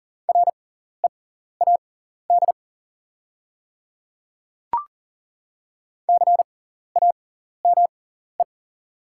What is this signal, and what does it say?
Morse code, heard as a keyed tone, about 40 wpm, spaced as for 14 wpm, 690 Hz. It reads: READ CAME
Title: 6x Character Spacing - Top 200 Words - 40wpm